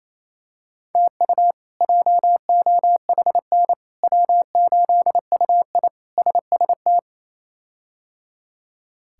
Morse code, heard as a keyed tone, about 27 wpm, 700 Hz. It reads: TU JOHN W8UI SST
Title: K1USN SST QSO Practice - 28wpm